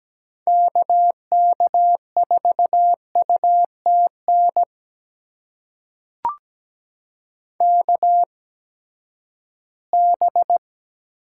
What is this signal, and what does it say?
Morse code, heard as a keyed tone, about 17 wpm, spaced as for 12 wpm, 700 Hz. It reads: KK4UTN KB
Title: USA Call Signs - 6 Characters - 8x Character Spacing - 17wpm